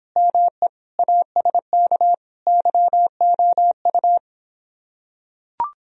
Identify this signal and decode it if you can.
Morse code, heard as a keyed tone, about 26 wpm, 700 Hz. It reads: ME ASK YOU